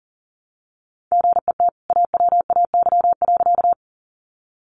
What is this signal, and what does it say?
Morse code, heard as a keyed tone, about 40 wpm, 700 Hz. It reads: GET AWAY.
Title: Sentences from Top 600 Words 40wpm